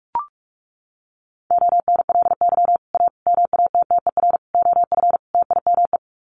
Morse code, heard as a keyed tone, about 44 wpm, 700 Hz. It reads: ONLY A MATTER OF TIME
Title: Sets of 5 Words Encore 45wpm